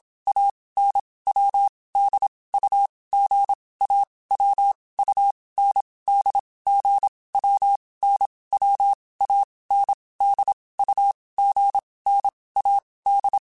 Morse code, heard as a keyed tone, about 27 wpm, 800 Hz. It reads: A N W D U G A W U N D G W N W A N D U G N A D
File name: Basic Sets / Basic Letter Groups Collective7 / AWUNDG@25